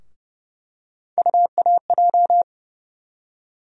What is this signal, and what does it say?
Morse code, heard as a keyed tone, about 30 wpm, 700 Hz. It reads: UAJ
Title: USA Call Sign Postfixes - 3 Letters 30wpm